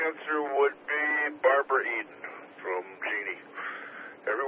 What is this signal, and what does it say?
Emergency Alert System (SAME) tones, heard as harsh buzzing bursts, with speech. It is heard from 0.8 to 3.6 s.